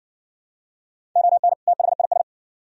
Morse code, heard as a keyed tone, about 56 wpm, 700 Hz. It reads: ON THIS